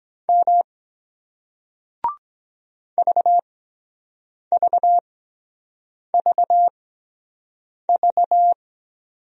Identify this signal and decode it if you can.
Morse code, heard as a keyed tone, about 23 wpm, spaced as for 7 wpm, 700 Hz. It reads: M VVVV